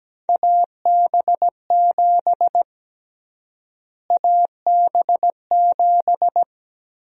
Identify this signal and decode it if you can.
Morse code, heard as a keyed tone, about 17 wpm, 700 Hz. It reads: AB7 AB7
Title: USA Call Sign Prefixes - Repeated 2x - 17wpm